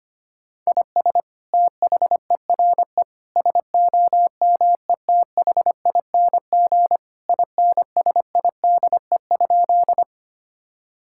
Morse code, heard as a keyed tone, about 25 wpm, 700 Hz. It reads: IS THERE SOMETHING INSIDE?
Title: Sentences from Top 1000 Words - Review Part 6 - 25wpm